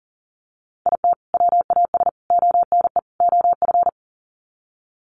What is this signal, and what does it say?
Morse code, heard as a keyed tone, about 40 wpm, 700 Hz. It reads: IT WAS ONE OF
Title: Sets of 4 Words 40wpm